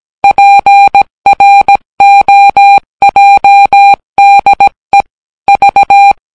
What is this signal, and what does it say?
Morse code, heard as a keyed tone, about 17 wpm, 800 Hz. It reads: PROJDE V